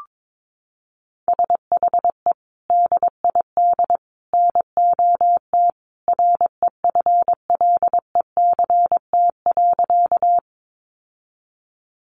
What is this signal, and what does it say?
Morse code, heard as a keyed tone, about 22 wpm, 700 Hz. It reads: SHE DID NOT REFLECT.
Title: Sentences from Top 1500 Words 22wpm